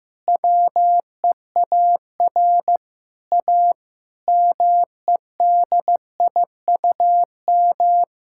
Morse code, heard as a keyed tone, about 15 wpm, 700 Hz. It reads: WEAR A MEDIUM
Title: Sentences from Top 1000 Words - Review Part 6 - 15wpm